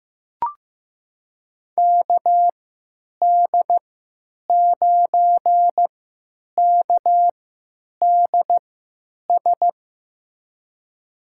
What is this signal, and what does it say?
Morse code, heard as a keyed tone, about 15 wpm, spaced as for 9 wpm, 700 Hz. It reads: KD9KDS
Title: USA Call Signs - 6 Characters - 3x Character Spacing - 15wpm